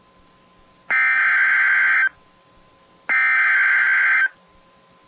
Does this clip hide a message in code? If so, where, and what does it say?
Yes: Emergency Alert System (SAME) tones (header bursts) from 0.9 to 4.2 s, heard as harsh buzzing bursts.